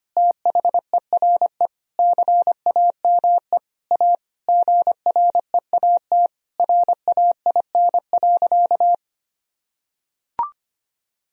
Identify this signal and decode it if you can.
Morse code, heard as a keyed tone, about 25 wpm, 700 Hz. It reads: THERE CAME A GREAT RAIN.
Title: Sentences from Top 1000 Words - Review Part 3 - 25wpm